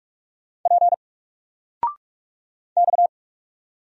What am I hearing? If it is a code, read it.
Morse code, heard as a keyed tone, about 44 wpm, 700 Hz. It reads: P X
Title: Single Letter-Number 45wpm